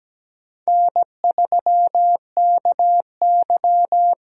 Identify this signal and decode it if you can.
Morse code, heard as a keyed tone, about 17 wpm, 700 Hz. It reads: N3KY
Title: USA Call Signs - 4 Characters - Repeated 2x - 17wpm